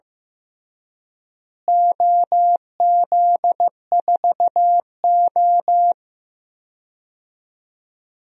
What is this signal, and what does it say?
Morse code, heard as a keyed tone, about 15 wpm, 700 Hz. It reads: OZ4O
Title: DXCC-100 Call Signs - 4 Characters - Repeated 2x - 15wpm